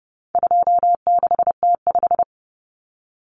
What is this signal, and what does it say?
Morse code, heard as a keyed tone, about 30 wpm, 700 Hz. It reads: 26T5
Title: Common Cut Numbers - 4-Digit 30wpm